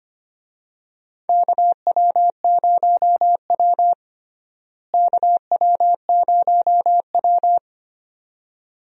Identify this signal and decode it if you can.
Morse code, heard as a keyed tone, about 25 wpm, 700 Hz. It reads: KW0W KW0W
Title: USA Call Signs - 4 Characters - Repeated 2x - 25wpm